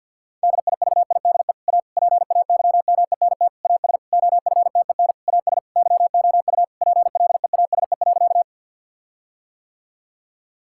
Morse code, heard as a keyed tone, about 51 wpm, 700 Hz. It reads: DIVIDE A PAYMENT AS OFTEN AS YOU PLEASE.
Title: Sentences from Top 1600 Words 50wpm